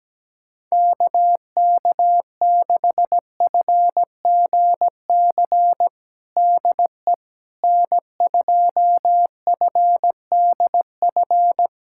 Morse code, heard as a keyed tone, about 17 wpm, 700 Hz. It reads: KK6FGC DE N2FDF
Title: USA Callsign DE Callsign 17wpm